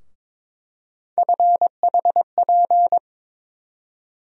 Morse code, heard as a keyed tone, about 22 wpm, 700 Hz. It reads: FHP
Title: USA Call Sign Postfixes - 3 Letters 22wpm